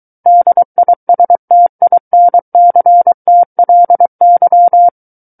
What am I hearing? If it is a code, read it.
Morse code, heard as a keyed tone, about 23 wpm, 700 Hz. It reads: DISTINCTLY
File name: Book One / 14-Chapter 14 - 23 WPM